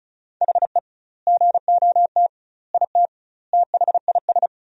Morse code, heard as a keyed tone, about 36 wpm, 700 Hz. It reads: HE GOT IT THIS
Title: 2x Word-Spacing - Sentences from Top 400 Words - 35wpm